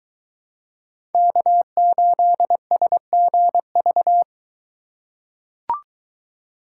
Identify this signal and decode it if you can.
Morse code, heard as a keyed tone, about 22 wpm, 700 Hz. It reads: K8SGV